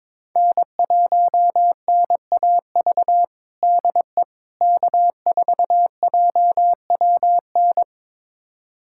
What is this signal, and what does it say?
Morse code, heard as a keyed tone, about 22 wpm, 700 Hz. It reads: N1NAV DE K4JWN
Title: USA Callsign DE Callsign - 5 Characters - 22wpm